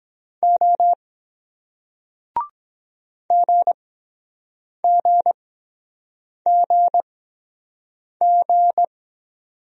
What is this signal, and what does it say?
Morse code, heard as a keyed tone, about 22 wpm, spaced as for 7 wpm, 700 Hz. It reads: O GGGG